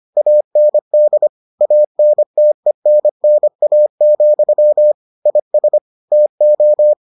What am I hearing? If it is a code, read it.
Morse code, heard as a keyed tone, about 25 wpm, 600 Hz. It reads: AND ANTENNA, IS TO